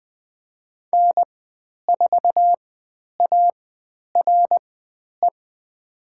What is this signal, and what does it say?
Morse code, heard as a keyed tone, about 20 wpm, spaced as for 10 wpm, 700 Hz. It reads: N4ARE
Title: 095 Lesson 16g - Letter c - Call Signs 20wpm